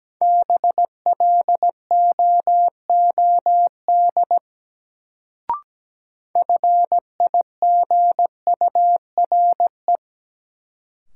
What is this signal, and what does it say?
Morse code, heard as a keyed tone, about 17 wpm, 700 Hz. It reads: BLOOD FIGURE
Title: Sentences from Top 1000 Words - Review Part 2 - New Word Drill - 17wpm